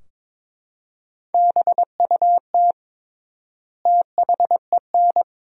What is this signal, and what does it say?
Morse code, heard as a keyed tone, about 22 wpm, 700 Hz. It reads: BUT THEN
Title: 3x Word-Spacing - Sentences from Top 300 Words - 22wpm